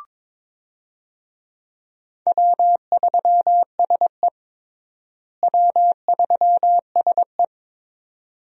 Morse code, heard as a keyed tone, about 21 wpm, 700 Hz. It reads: W3SE W3SE
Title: USA Call Signs - 4 Characters - Repeated 2x - 22wpm